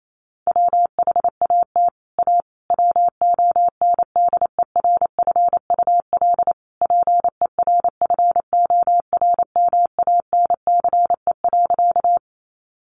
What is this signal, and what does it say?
Morse code, heard as a keyed tone, about 28 wpm, 700 Hz. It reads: WHAT A WONDERFUL PERFORMANCE.
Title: Sentences from Top 1100 Words 28wpm